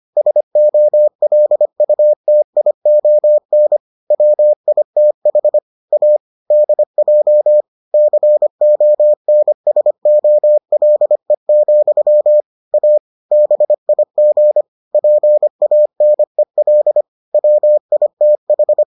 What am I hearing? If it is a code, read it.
Morse code, heard as a keyed tone, about 25 wpm, 600 Hz. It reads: SOLUTION WITH A DJ CONSOLE, A BIG PANEL WITH